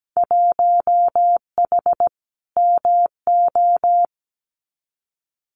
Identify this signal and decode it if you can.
Morse code, heard as a keyed tone, about 17 wpm, 700 Hz. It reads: 1H MO
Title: Winter Field Day Exchanges 17wpm